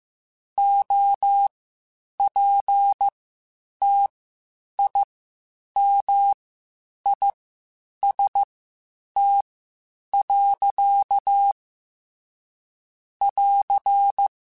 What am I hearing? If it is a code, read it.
Morse code, heard as a keyed tone, about 15 wpm, spaced as for 8 wpm, 800 Hz. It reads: OPTIMIST. <AR>
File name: Quotes Spaced / Quotes Spaced-mp3s / Optimist sp @13